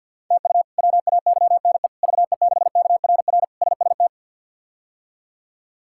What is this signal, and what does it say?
Morse code, heard as a keyed tone, about 47 wpm, 700 Hz. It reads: TU WAYNE VE6KRR SST
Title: K1USN SST QSO Practice - 50wpm